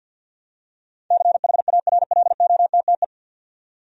Morse code, heard as a keyed tone, about 52 wpm, 700 Hz. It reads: CHARLOTTE
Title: USA Common Names - 50wpm